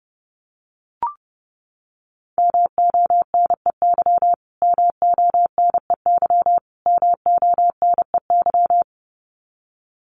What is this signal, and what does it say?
Morse code, heard as a keyed tone, about 30 wpm, 700 Hz. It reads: MONEY MONEY MONEY
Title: Top 500 Words Repeated 30wpm